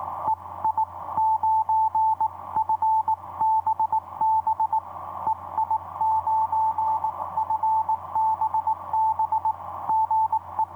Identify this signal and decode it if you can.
Morse code, heard as a keyed tone, about 23 wpm, spaced as for 19 wpm, 880 Hz, with radio noise. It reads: EI9FBB EI9FBB GE
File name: h74leon ei9fbb etc 30m cw 10nov2010 2303